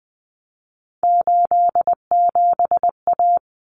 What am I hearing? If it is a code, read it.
Morse code, heard as a keyed tone, about 20 wpm, 700 Hz. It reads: 87A
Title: Common Cut Numbers - 3-Digit 20wpm